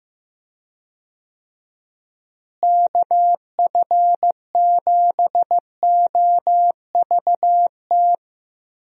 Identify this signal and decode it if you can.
Morse code, heard as a keyed tone, about 15 wpm, 700 Hz. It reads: KF7OVT
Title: USA Call Signs - 6 Characters - Repeated 2x - 15wpm